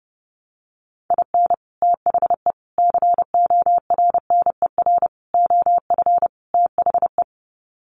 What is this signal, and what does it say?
Morse code, heard as a keyed tone, about 30 wpm, 700 Hz. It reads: IN THE CORNER OF THE
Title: Sets of 5 Words Encore 30wpm